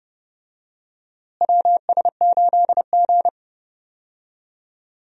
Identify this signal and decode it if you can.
Morse code, heard as a keyed tone, about 30 wpm, 700 Hz. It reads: WS8G
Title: USA Call Signs - 4 Characters - Repeated 3x - 30wpm